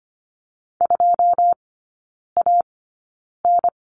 Morse code, heard as a keyed tone, about 25 wpm, spaced as for 9 wpm, 700 Hz. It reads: 2AN
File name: Lesson 14e - Number 2 - Three Characters 25wpm